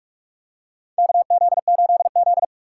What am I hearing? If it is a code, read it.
Morse code, heard as a keyed tone, about 44 wpm, 700 Hz. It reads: KZ8Z
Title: USA Call Signs - 4 Characters - 6x Character Spacing - 45wpm